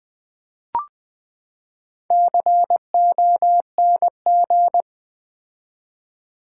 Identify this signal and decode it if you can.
Morse code, heard as a keyed tone, about 20 wpm, 700 Hz. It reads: CONG
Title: Build A Word - 4 Letters At A Time - 12-Letter Words - 20wpm